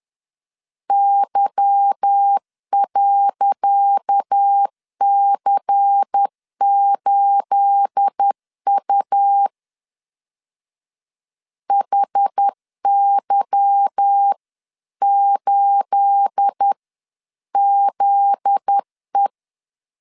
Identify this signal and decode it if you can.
Morse code, heard as a keyed tone, about 10 wpm, 800 Hz. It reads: Y.C8U HY8ZE
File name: LEZIONI / 10 SIGNS / 10C DOT 10wpm 2940